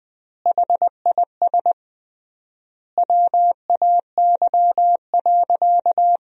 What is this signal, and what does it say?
Morse code, heard as a keyed tone, about 21 wpm, 700 Hz. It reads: HIS WAY.
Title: 3x Word-Spacing - Sentences from Top 300 Words - 20wpm